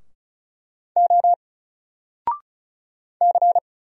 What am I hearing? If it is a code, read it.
Morse code, heard as a keyed tone, about 36 wpm, 700 Hz. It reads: O C
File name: Single-Letter-Number-35wpm